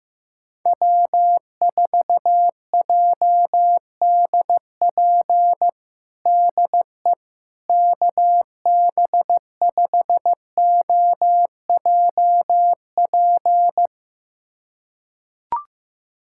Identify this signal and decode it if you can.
Morse code, heard as a keyed tone, about 16 wpm, 700 Hz. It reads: W4JDP DE KB5OJP
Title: USA Callsign DE Callsign 15wpm